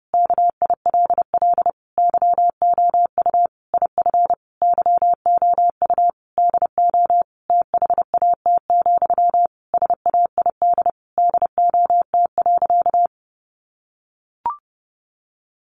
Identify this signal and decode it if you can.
Morse code, heard as a keyed tone, about 30 wpm, 700 Hz. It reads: KILL YOU IF YOU DO THAT, SAID DOT.
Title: Sentences from Top 1500 Words 30wpm